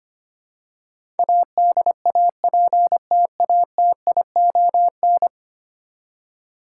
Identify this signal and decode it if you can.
Morse code, heard as a keyed tone, about 25 wpm, 700 Hz. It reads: ADAPTATION
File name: 10 Letter Words - First 3 Letters Given - 25wpm